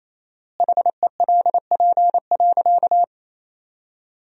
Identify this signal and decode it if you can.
Morse code, heard as a keyed tone, about 28 wpm, 700 Hz. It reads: HELP.
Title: 4x Word-Spacing - Sentences from Top 300 Words - 28wpm